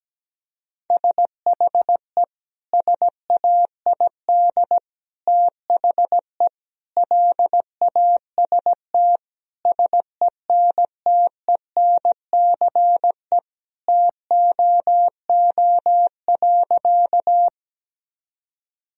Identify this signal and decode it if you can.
Morse code, heard as a keyed tone, about 17 wpm, 700 Hz. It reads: SHE SAID THE LAST SENTENCE TOO.
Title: Sentences from Top 1000 Words - Review Part 1 - 17wpm